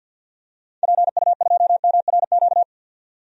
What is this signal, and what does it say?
Morse code, heard as a keyed tone, about 53 wpm, 700 Hz. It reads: PU2MRQ
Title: DXCC-100 Call Signs - 6 Characters - 50wpm